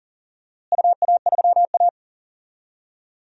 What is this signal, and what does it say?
Morse code, heard as a keyed tone, about 39 wpm, 700 Hz. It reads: UA3A
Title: DXCC-100 Call Signs - 4 Characters - Repeated 2x - 40wpm